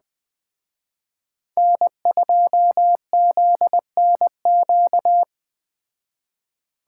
Morse code, heard as a keyed tone, about 20 wpm, 700 Hz. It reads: N2ZNQ